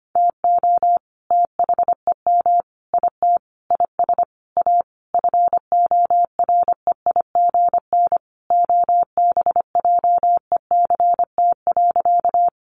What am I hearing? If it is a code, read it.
Morse code, heard as a keyed tone, about 25 wpm, 700 Hz. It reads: TO THEM IT IS A FOREIGN OBJECT.
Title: Sentences from Top 1400 Words 25wpm